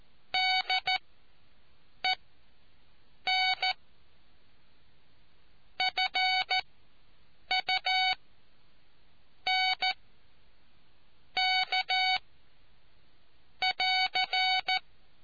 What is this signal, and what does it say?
Morse code, heard as a keyed tone, about 14 wpm, spaced as for 6 wpm, 750 Hz. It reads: DEN FUNK <AR>